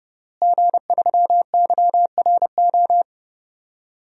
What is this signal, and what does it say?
Morse code, heard as a keyed tone, about 30 wpm, 700 Hz. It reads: G3YRO
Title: DXCC-100 Call Signs - 5 Characters - Repeated 3x - 30wpm